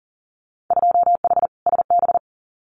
Morse code, heard as a keyed tone, about 40 wpm, 700 Hz. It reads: 2H SB
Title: Winter Field Day Exchanges Encore 40wpm